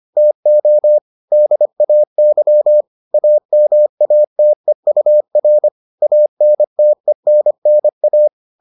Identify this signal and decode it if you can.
Morse code, heard as a keyed tone, about 25 wpm, 600 Hz. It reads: TO DAY AMATEUR ANTENNA